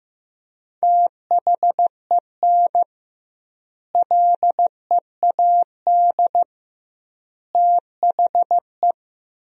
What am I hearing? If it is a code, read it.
Morse code, heard as a keyed tone, about 15 wpm, 700 Hz. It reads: THEN LEAD THE
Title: 2x Word-Spacing - Sentences from Top 500 Words - 15wpm